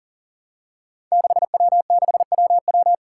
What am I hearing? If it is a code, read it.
Morse code, heard as a keyed tone, about 40 wpm, 700 Hz. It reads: 6W6WW
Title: Five Challenging Letter-Number Combinations 40wpm